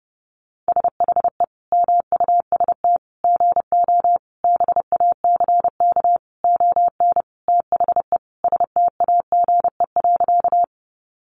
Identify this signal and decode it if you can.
Morse code, heard as a keyed tone, about 30 wpm, 700 Hz. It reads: SHE MUST GO BACK ON THE STAGE.
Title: Sentences from Top 1200 Words 30wpm